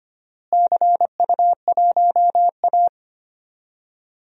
Morse code, heard as a keyed tone, about 25 wpm, 700 Hz. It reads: CU1A
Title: DXCC-100 Call Signs - 4 Characters - Repeated 3x - 25wpm